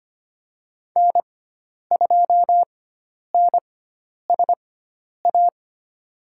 Morse code, heard as a keyed tone, about 25 wpm, spaced as for 10 wpm, 700 Hz. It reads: N2NSA
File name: Lesson 16g - Letter c - Call Signs 25wpm